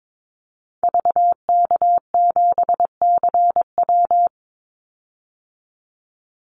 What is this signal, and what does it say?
Morse code, heard as a keyed tone, about 22 wpm, 690 Hz. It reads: VK7CW
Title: Call Signs Repeated 2x - 22wpm